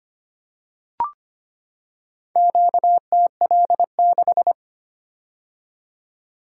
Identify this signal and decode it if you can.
Morse code, heard as a keyed tone, about 25 wpm, 700 Hz. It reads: QTL6
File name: practice / 25wpm / Four Letter-Number Combinations - 25wpm